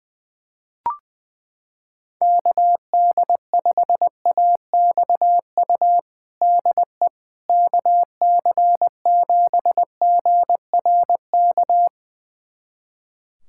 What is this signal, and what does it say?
Morse code, heard as a keyed tone, about 20 wpm, 700 Hz. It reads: KD5AXU DE KC7GRK
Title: USA Callsign DE Callsign - 6 Characters - 20wpm